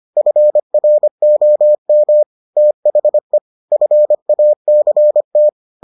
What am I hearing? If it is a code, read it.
Morse code, heard as a keyed tone, about 25 wpm, 600 Hz. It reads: FROM THE FACT